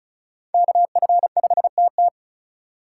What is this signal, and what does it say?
Morse code, heard as a keyed tone, about 43 wpm, 700 Hz. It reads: KF5TT
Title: Speed Racing - USA 2x2 Call Signs 35wpm